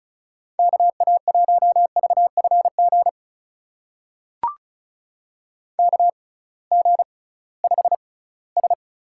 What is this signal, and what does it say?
Morse code, heard as a keyed tone, about 36 wpm, spaced as for 20 wpm, 700 Hz. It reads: KA1VFG KG5S
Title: USA Call Signs - 6 Characters - 6x Character Spacing - 35wpm